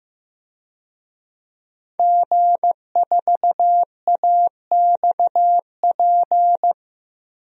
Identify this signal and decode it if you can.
Morse code, heard as a keyed tone, about 15 wpm, 700 Hz. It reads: G4AXP
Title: DXCC-100 Call Signs - 5 Characters - Repeated 2x - 15wpm